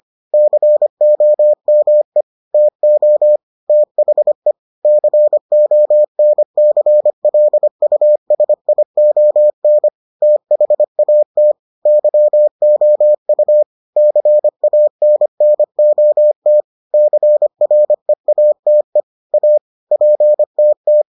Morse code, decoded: COME TO THE CONCLUSION THAT YOU CANNOT CREATE A PTT

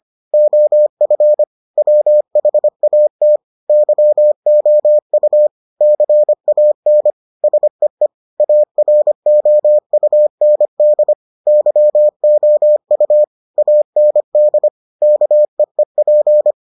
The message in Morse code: OF WHAT YOU CAN SEE AROUND YOU AND KEEP

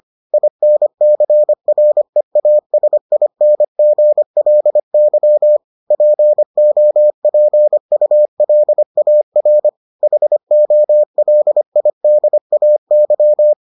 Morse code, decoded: INCREASINGLY POPULAR HOLIDAY

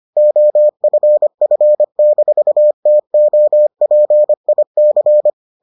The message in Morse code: OFF-TOPIC

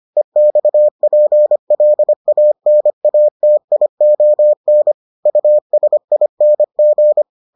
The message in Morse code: EXPLANATION USING

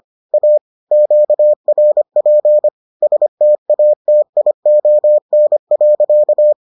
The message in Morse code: A QRP STATION.

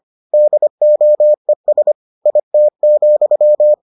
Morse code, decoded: DOES IT,

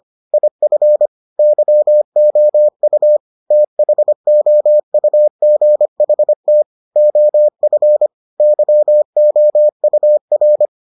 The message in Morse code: IF YOU THOUGHT OF YOUR

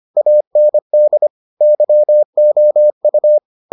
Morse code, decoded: AND YOU

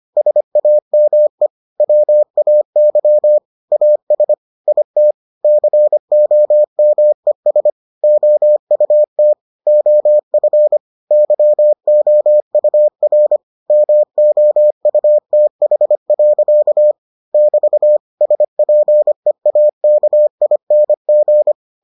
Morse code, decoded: SAME WAY AS IT COMES OUT OF YOUR MOUTH. <BT> SPEAKING

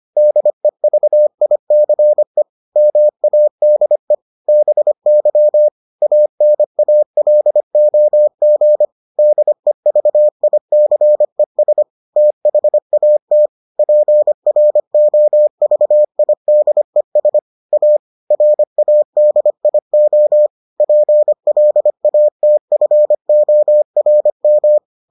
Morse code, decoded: DEVICE MADE BY ANALOG DEVICES THAT PROVIDES A RADIO PLATFORM

